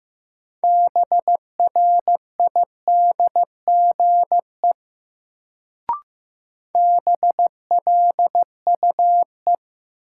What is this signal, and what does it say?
Morse code, heard as a keyed tone, about 15 wpm, 700 Hz. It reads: BRIDGE BLUE
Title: Sentences from Top 1200 Words - New Word Drill Alt - 15wpm